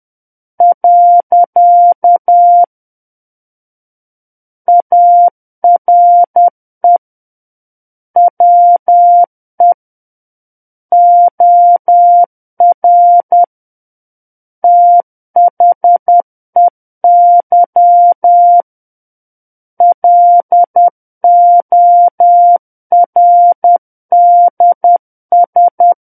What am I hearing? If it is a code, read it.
Morse code, heard as a keyed tone, about 10 wpm, 700 Hz. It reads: . ARE WE OR THEY LORDS
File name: Book One / 01-Chapter 1 - 10 WPM